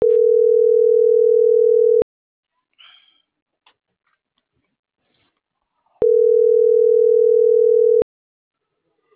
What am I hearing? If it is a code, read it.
Ringback tone (North America), 440 + 480 Hz.